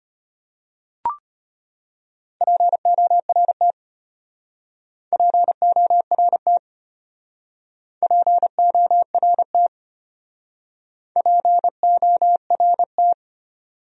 Morse code, heard as a keyed tone, about 33 wpm, 700 Hz. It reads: PORT PORT PORT PORT